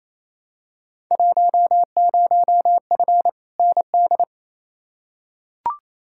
Morse code, heard as a keyed tone, about 28 wpm, 700 Hz. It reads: 10F ND
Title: Field Day Exchanges Encore - 28wpm